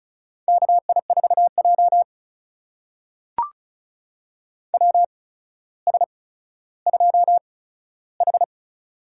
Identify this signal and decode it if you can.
Morse code, heard as a keyed tone, about 36 wpm, spaced as for 15 wpm, 700 Hz. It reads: KI4J WS2H
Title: USA Call Signs - 4 Characters - 8x Character Spacing - 35wpm